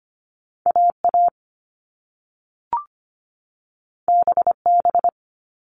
Morse code, heard as a keyed tone, about 25 wpm, 700 Hz. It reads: AA BB
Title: All Cut Numbers - 2-Digit 25wpm